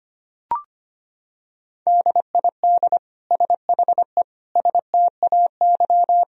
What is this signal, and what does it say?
Morse code, heard as a keyed tone, about 25 wpm, 700 Hz. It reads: DID SHE STAY
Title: Sentences from Top 500 Words and Abbreviations 25wpm